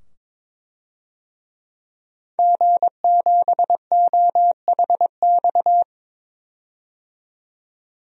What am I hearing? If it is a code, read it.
Morse code, heard as a keyed tone, about 22 wpm, 700 Hz. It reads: G7OHX
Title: DXCC-100 Call Signs - 5 Characters - 22wpm